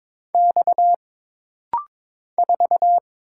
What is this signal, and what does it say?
Morse code, heard as a keyed tone, about 22 wpm, 700 Hz. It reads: X 4
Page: This rapid-fire timing is geared to push your ICR (Instant Character Recognition) skills.